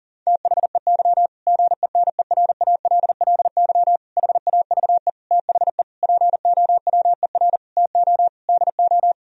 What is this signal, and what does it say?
Morse code, heard as a keyed tone, about 40 wpm, 700 Hz. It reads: THEY GENERALLY HAVE THE POWER TO DO
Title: Sentences from Top 1000 Words - Review Part 4 - 40wpm